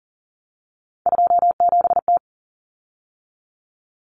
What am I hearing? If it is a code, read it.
Morse code, heard as a keyed tone, about 40 wpm, 700 Hz. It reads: 27T
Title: Common Cut Numbers - 3-Digit 40wpm